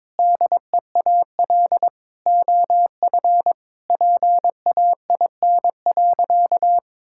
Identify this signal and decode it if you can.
Morse code, heard as a keyed tone, about 21 wpm, 700 Hz. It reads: DEAL OF PAIN.
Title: Sentences from Top 1600 Words 22wpm